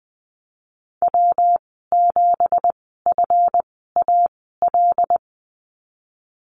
Morse code, heard as a keyed tone, about 20 wpm, spaced as for 15 wpm, 700 Hz. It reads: W7FAL
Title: 193 Lesson 31g - Letter v - Call Signs 20wpm